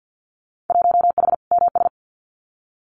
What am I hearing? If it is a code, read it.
Morse code, heard as a keyed tone, about 50 wpm, 700 Hz. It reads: 1H MS
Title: Winter Field Day Exchanges 50wpm